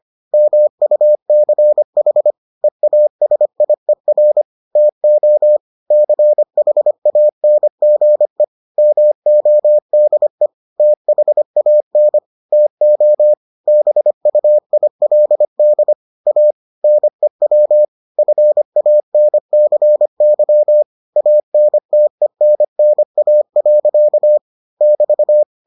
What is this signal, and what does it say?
Morse code, heard as a keyed tone, about 25 wpm, 600 Hz. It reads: MUCH EASIER TO CHANGE MODE THAN TO BUILD A NEW FANCY ANTENNA. <BT>